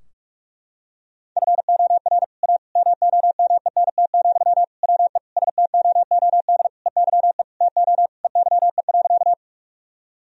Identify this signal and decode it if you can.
Morse code, heard as a keyed tone, about 46 wpm, 700 Hz. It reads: FOR A MOMENT, WE STOOD EYE TO EYE.
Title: Sentences from Top 1000 Words - Review Part 2 - 45wpm